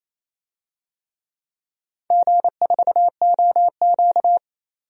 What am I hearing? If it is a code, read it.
Morse code, heard as a keyed tone, about 28 wpm, 700 Hz. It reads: G4OQ
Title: DXCC-100 Call Signs - 4 Characters - 28wpm